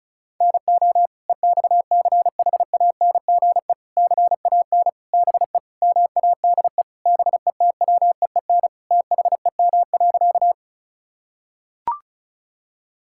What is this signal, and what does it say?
Morse code, heard as a keyed tone, about 36 wpm, 700 Hz. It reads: NO EXCHANGE CAN BE MADE BETWEEN THEM.
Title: Sentences from Top 1000 Words - Review Part 6 - 35wpm